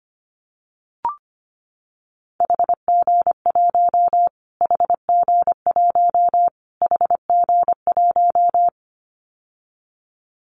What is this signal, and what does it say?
Morse code, heard as a keyed tone, about 25 wpm, 700 Hz. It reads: HG1 HG1 HG1
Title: Callsign Prefixes for Top 25 DXCC - Repeated 3x - 25wpm